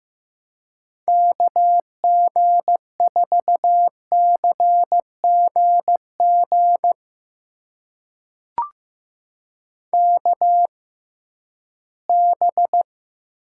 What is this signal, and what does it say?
Morse code, heard as a keyed tone, about 15 wpm, spaced as for 11 wpm, 700 Hz. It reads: KG4CGG KB
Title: USA Call Signs - 6x Character Spacing - 15wpm